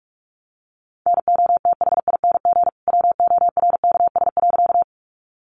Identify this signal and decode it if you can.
Morse code, heard as a keyed tone, about 44 wpm, 700 Hz. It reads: NOTHING WORKS.
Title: Additional Easy Phrases 45wpm